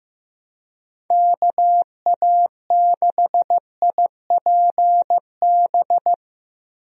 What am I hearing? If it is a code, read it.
Morse code, heard as a keyed tone, about 16 wpm, 700 Hz. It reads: KA6IPB
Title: USA Call Signs - 6 Characters - Repeated 2x - 15wpm